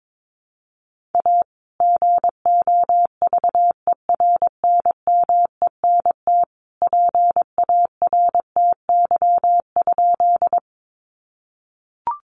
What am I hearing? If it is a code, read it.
Morse code, heard as a keyed tone, about 22 wpm, 700 Hz. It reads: A GOVERNMENT PARTY?
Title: Sentences from Top 1400 Words 22wpm